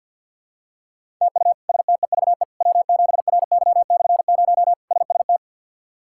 Morse code, heard as a keyed tone, about 51 wpm, 690 Hz. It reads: TU STEVE W7RY/0 SST
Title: K1USN SST QSO Practice - 50wpm